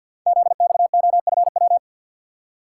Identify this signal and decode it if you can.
Morse code, heard as a keyed tone, about 50 wpm, 700 Hz. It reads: ZXOFW
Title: Code Groups - 5 Letter:Numbers - 50wpm